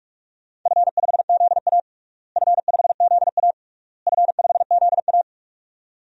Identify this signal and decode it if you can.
Morse code, heard as a keyed tone, about 41 wpm, 700 Hz. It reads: F5ZA F5ZA F5ZA
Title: DXCC-100 Call Signs - 4 Characters - Repeated 3x - 45wpm